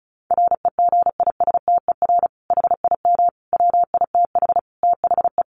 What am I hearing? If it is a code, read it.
Morse code, heard as a keyed tone, about 34 wpm, 700 Hz. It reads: REGISTER HIM WITH THE
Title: Sentences from Top 1200 Words 35wpm